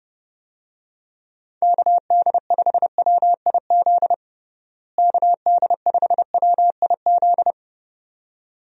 Morse code, heard as a keyed tone, about 32 wpm, 700 Hz. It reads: KD5WIZ KD5WIZ
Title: USA Call Signs - 6 Characters - Repeated 2x - 30wpm